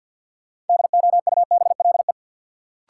Morse code, heard as a keyed tone, about 53 wpm, 700 Hz. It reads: DOUBLE
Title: Sentences from Top 1400 Words - New Word Drill Alt - 50wpm